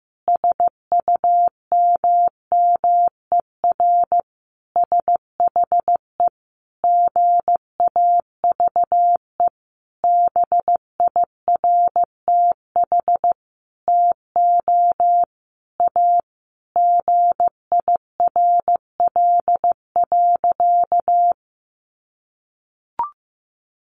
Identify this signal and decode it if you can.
Morse code, heard as a keyed tone, about 14 wpm, 700 Hz. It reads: SUMMER SHE GAVE BIRTH TO A GIRL.
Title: Sentences from Top 1100 Words 15wpm